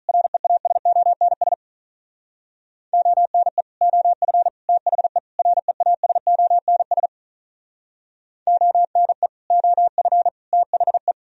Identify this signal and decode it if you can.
Morse code, heard as a keyed tone, about 44 wpm, 700 Hz. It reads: REASONS ONE OF THE REASONS ONE OF THE
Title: Speed-Racing - Sets of 4 Words 35wpm